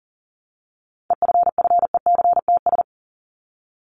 Morse code, heard as a keyed tone, about 40 wpm, 700 Hz. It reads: EFFECTS
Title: ES:S Suffix - Top 100 Words with Base Words 40wpm